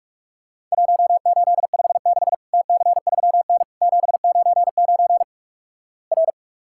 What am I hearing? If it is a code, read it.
Morse code, heard as a keyed tone, about 44 wpm, 700 Hz. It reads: 1856 TC3N 799 R
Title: Worked All Europe DX Contest - QTC Exchange - 45wpm